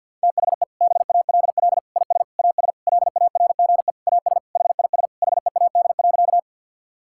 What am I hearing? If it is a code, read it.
Morse code, heard as a keyed tone, about 52 wpm, 700 Hz. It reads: THE BALL IS AS LARGE AS HIS HEAD.
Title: Sentences from Top 1000 Words - Review Part 6 - 50wpm